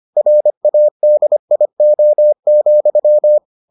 Morse code, heard as a keyed tone, about 25 wpm, 600 Hz. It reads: RADIO,